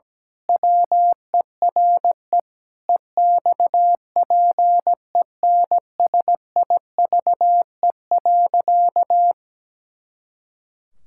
Morse code, heard as a keyed tone, about 17 wpm, 700 Hz. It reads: WERE EXPENSIVE.
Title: Pangram Challenge v2 - 17wpm